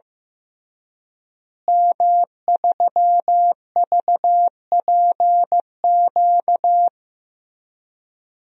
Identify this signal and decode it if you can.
Morse code, heard as a keyed tone, about 15 wpm, 700 Hz. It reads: M3VPQ
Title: DXCC-100 Call Signs - 5 Characters - Repeated 3x - 15wpm